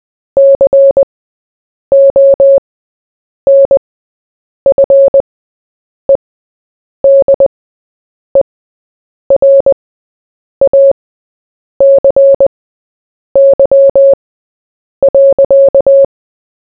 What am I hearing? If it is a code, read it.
Morse code, heard as a keyed tone, about 20 wpm, spaced as for 8 wpm, 550 Hz. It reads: CONFEDERACY.
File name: Short-Story-4-8wpm